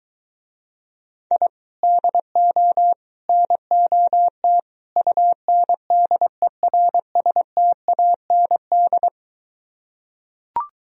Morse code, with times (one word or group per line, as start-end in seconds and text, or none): 1.3-1.5 I
1.8-2.9 DO
3.3-4.6 NOT
5.0-9.1 UNDERSTAND